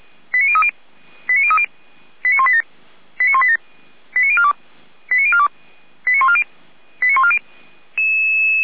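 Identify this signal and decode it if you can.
Five-tone selective call (ZVEI), heard as quick runs of five tones, tones of about 70 ms. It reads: 89030 89030 89187 89187 89042 89042 89140 89140
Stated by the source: ZVEI five-tone selective call 89299, received by SDR.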